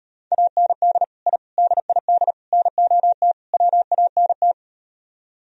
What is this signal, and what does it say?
Morse code, heard as a keyed tone, about 38 wpm, 700 Hz. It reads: AND I DID NOT WANT